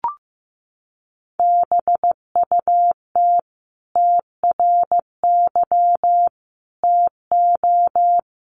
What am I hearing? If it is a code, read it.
Morse code, heard as a keyed tone, about 15 wpm, 700 Hz. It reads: BUT TRY TO
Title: Sentences from Top 1300 Words 15wpm